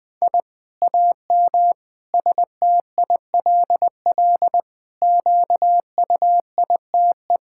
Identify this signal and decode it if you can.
Morse code, heard as a keyed tone, about 21 wpm, 700 Hz. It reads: I AM STILL QUITE
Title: Sentences from Top 1000 Words - Review Part 5 - 20wpm